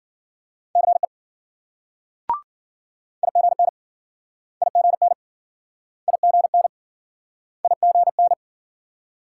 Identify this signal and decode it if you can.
Morse code, heard as a keyed tone, about 54 wpm, 700 Hz. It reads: CE IGN IGN IGN IGN